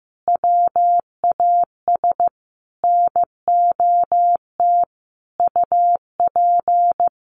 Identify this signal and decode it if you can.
Morse code, heard as a keyed tone, about 15 wpm, 700 Hz. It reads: WAS NOT UP
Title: Sentences from Top 1200 Words 15wpm